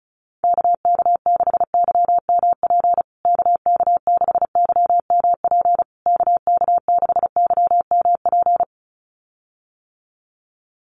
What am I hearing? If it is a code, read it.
Morse code, heard as a keyed tone, about 34 wpm, 700 Hz. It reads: KK6YMP KK6YMP KK6YMP
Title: USA Call Signs Repeated 3x 35wpm